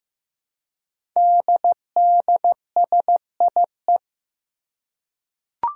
Morse code, heard as a keyed tone, about 15 wpm, 700 Hz. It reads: DDSIE
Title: Five Challenging Letter-Number Combinations 15wpm